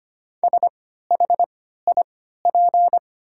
Morse code, heard as a keyed tone, about 25 wpm, spaced as for 14 wpm, 700 Hz. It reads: SHIP
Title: 3x Character Spacing - Top 500 Words - 25wpm